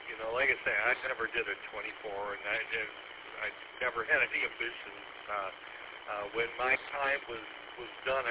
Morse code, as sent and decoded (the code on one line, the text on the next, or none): none